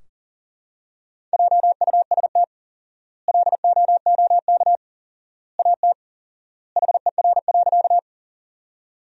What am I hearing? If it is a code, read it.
Morse code, heard as a keyed tone, about 40 wpm, 700 Hz. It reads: JUST LOOK AT HER.
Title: 4x Word-Spacing - Sentences from Top 200 Words - 40wpm